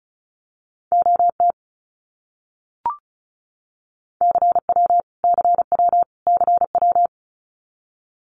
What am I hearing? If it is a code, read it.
Morse code, heard as a keyed tone, about 34 wpm, 700 Hz. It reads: OT CW CW CW
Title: Abbreviations Repeated 3x - 35wpm